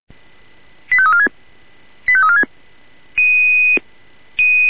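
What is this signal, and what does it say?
Five-tone selective call (ZVEI), heard as quick runs of five tones, tones of about 70 ms. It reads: 85346 85346
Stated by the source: ZVEI five-tone selective call received on an SDR.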